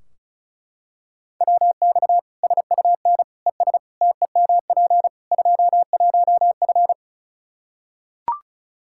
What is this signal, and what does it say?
Morse code, heard as a keyed tone, about 36 wpm, 700 Hz. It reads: WX SUN ES TEMP 21F
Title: Practice QSO - Exchange 2 - Info - 35wpm